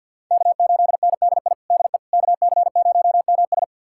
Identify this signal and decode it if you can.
Morse code, heard as a keyed tone, about 52 wpm, 700 Hz. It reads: K8NDI DE KC0MS
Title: USA Callsign DE Callsign - 5 Characters - 50wpm